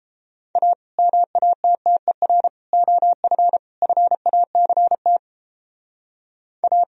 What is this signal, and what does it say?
Morse code, heard as a keyed tone, about 31 wpm, 700 Hz. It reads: A MATTER OF FACT A